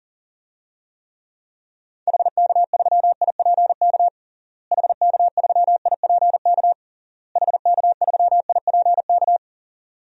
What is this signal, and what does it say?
Morse code, heard as a keyed tone, about 40 wpm, 700 Hz. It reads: HK3IPK HK3IPK HK3IPK
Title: DXCC-100 Call Signs - 6 Characters - Repeated 3x - 40wpm